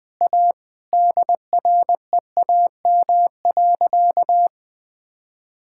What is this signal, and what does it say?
Morse code, heard as a keyed tone, about 20 wpm, 700 Hz. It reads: A DREAM.
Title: Sentences with 3-Word Phrases - 20wpm